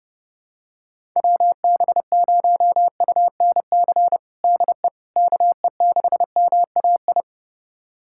Morse code, heard as a keyed tone, about 30 wpm, 700 Hz. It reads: WB0UNC DE KE6MAI
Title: USA Callsign DE Callsign 30wpm